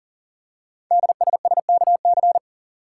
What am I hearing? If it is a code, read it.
Morse code, heard as a keyed tone, about 38 wpm, 690 Hz. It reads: DSSKC
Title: Code Groups - 5 Letters - 40wpm